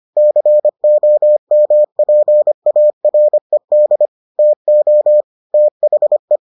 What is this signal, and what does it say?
Morse code, heard as a keyed tone, about 25 wpm, 600 Hz. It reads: COMPARED TO THE